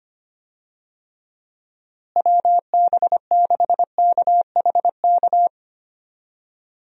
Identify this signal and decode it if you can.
Morse code, heard as a keyed tone, about 25 wpm, 700 Hz. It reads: WB6KHK
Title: USA Call Signs - 6 Characters - Repeated 3x - 25wpm